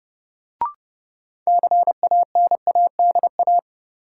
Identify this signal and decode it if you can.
Morse code, heard as a keyed tone, about 30 wpm, 700 Hz. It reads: CANADA
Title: Sets of 5 Words - New Word Drill - 30wpm